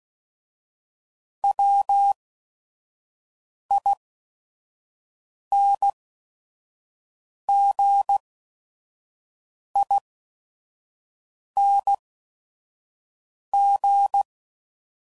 Morse code, heard as a keyed tone, about 16 wpm, spaced as for 5 wpm, 800 Hz. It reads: WINGING